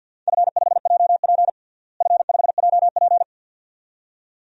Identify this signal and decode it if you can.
Morse code, heard as a keyed tone, about 50 wpm, 700 Hz. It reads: F5JP F5JP
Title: DXCC-100 Call Signs - 4 Characters - Repeated 2x - 50wpm